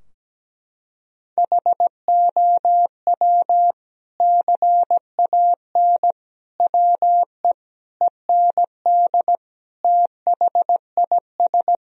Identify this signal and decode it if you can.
Morse code, heard as a keyed tone, about 17 wpm, 700 Hz. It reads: HOW CAN WE END THIS
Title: Sentences from Top 1000 Words - Review Part 2 - 17wpm